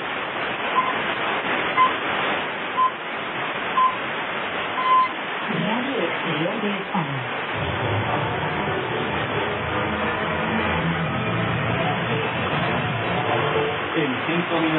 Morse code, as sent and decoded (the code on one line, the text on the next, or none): none